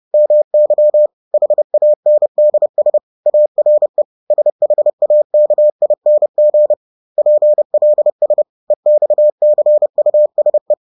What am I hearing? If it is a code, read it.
Morse code, heard as a keyed tone, about 30 wpm, 600 Hz. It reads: MY HANDS ARE SHAKING PLS EXCUSE
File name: 30wpm-7